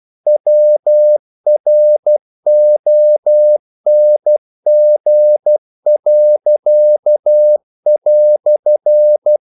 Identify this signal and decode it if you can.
Morse code, heard as a keyed tone, about 12 wpm, 600 Hz. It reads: WRONG."
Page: English quotations in CW (telegraphy - morse)